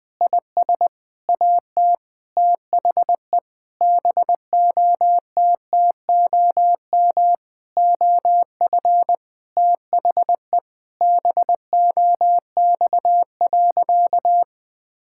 Morse code, heard as a keyed tone, about 19 wpm, 700 Hz. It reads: IS AT THE BOTTOM OF THE BOX.
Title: Sentences with 5-Word Phrases - 20wpm